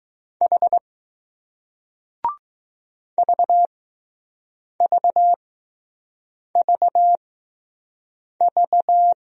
Morse code, heard as a keyed tone, about 20 wpm, spaced as for 6 wpm, 700 Hz. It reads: H VVVV